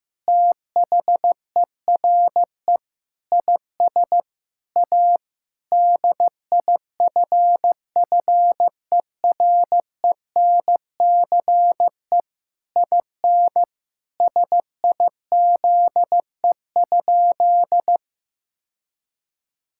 Morse code, heard as a keyed tone, about 15 wpm, 700 Hz. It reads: THERE IS A DIFFERENCE IN SIZE?
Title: Sentences from Top 1000 Words - Review Part 5 - 15wpm